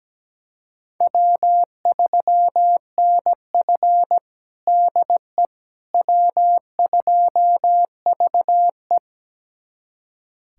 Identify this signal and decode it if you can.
Morse code, heard as a keyed tone, about 17 wpm, 700 Hz. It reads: W3NF DE W2VE
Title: USA Callsign DE Callsign - 4 Characters - 17wpm